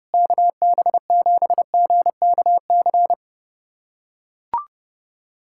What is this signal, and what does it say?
Morse code, heard as a keyed tone, about 30 wpm, 700 Hz. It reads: KB7GKC
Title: USA Callsign DE Callsign - 6 Characters - 30wpm